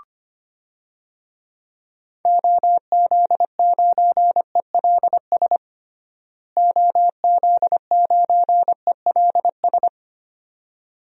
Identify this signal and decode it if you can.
Morse code, heard as a keyed tone, about 25 wpm, 700 Hz. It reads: OZ9ELS OZ9ELS
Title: DXCC-100 Call Signs - 6 Characters - Repeated 2x - 25wpm